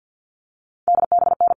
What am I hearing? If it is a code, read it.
Morse code, heard as a keyed tone, about 50 wpm, 700 Hz. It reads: DBN
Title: All Cut Numbers - 4-Digit 50wpm